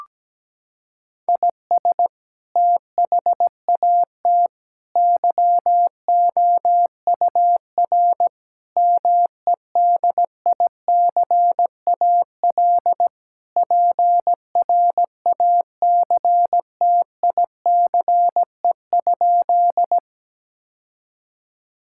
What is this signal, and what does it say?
Morse code, heard as a keyed tone, about 17 wpm, 700 Hz. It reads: IS THAT YOUR MEDICAL PRACTICE?
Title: Sentences from Top 1000 Words - Review Part 5 - 17wpm